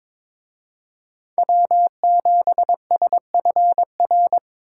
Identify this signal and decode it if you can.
Morse code, heard as a keyed tone, about 26 wpm, 700 Hz. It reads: W7SFR
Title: Speed Racing - USA 1x3 Call Signs 22wpm